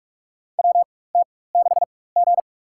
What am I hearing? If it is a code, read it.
Morse code, heard as a keyed tone, about 46 wpm, spaced as for 21 wpm, 700 Hz. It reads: WT6G
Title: USA Call Signs - 4 Characters - 4x Character Spacing - 45wpm